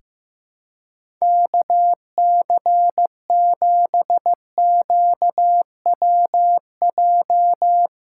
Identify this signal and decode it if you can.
Morse code, heard as a keyed tone, about 15 wpm, 700 Hz. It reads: KC7QWJ
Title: USA Callsign DE Callsign - 6 Characters - 15wpm